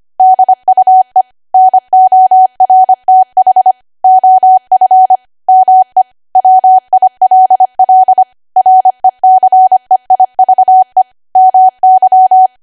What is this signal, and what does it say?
Morse code, heard as a keyed tone, about 25 wpm, 750 Hz. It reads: DUE NORTH OF ME WILL RECEIVE MY
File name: Archive / 25 WPM / 25WPM